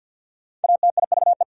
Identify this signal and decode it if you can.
Morse code, heard as a keyed tone, about 53 wpm, 690 Hz. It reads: ATIVE
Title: Prefixes and Suffixes 50wpm